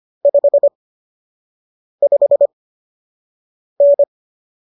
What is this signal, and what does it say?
Morse code, heard as a keyed tone, about 25 wpm, spaced as for 6 wpm, 570 Hz. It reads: 55N